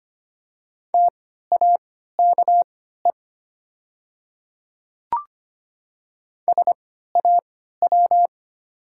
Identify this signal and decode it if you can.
Morse code, heard as a keyed tone, about 25 wpm, spaced as for 14 wpm, 700 Hz. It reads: TAKE SAW
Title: 3x Character Spacing - Top 300 Words - 25wpm